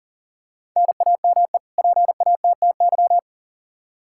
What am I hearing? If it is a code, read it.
Morse code, heard as a keyed tone, about 40 wpm, 700 Hz. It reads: NAME PATTY
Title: K1USN SST QSO Practice 40wpm